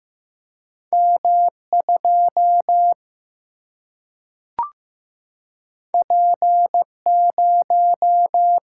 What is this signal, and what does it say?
Morse code, heard as a keyed tone, about 15 wpm, 700 Hz. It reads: M 2 P 0
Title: Single-Letter Single-Number 15wpm